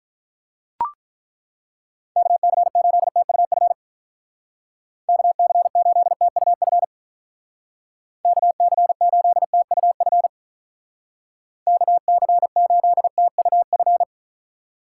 Morse code, heard as a keyed tone, about 43 wpm, 700 Hz. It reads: KC8TUF KC8TUF KC8TUF KC8TUF
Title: Speed Racing - USA 2x3 Call Signs 35wpm